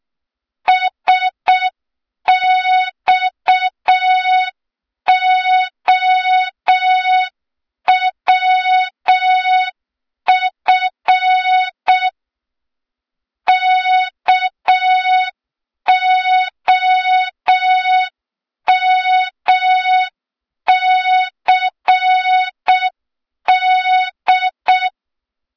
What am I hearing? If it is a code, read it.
Morse code, heard as a keyed tone, about 6 wpm, 3700 Hz, with noise. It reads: SXOWF KOMCD
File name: Morze-HA2MN / hang / 23 GYAKORLAT